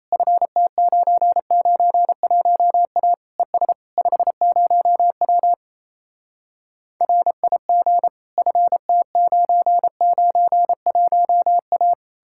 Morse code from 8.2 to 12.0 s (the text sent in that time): FT991A